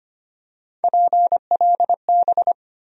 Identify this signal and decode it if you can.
Morse code, heard as a keyed tone, about 25 wpm, 700 Hz. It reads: PLB
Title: USA Call Sign Postfixes - 3 Letters 25wpm